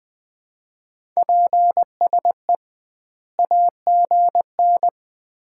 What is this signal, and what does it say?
Morse code, heard as a keyed tone, about 20 wpm, 700 Hz. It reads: PSE AGN
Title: Common QSO Phrases - 2x Word-Spacing - 20wpm